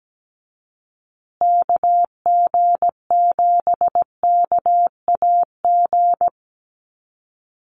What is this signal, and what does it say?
Morse code, heard as a keyed tone, about 17 wpm, 700 Hz. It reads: KG7KAG
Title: USA Call Signs Repeated 3x 17wpm